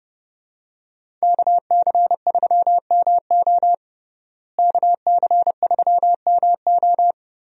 Morse code, heard as a keyed tone, about 30 wpm, 700 Hz. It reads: KC3MO KC3MO
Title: USA Call Signs - 5 Characters - Repeated 3x - 30wpm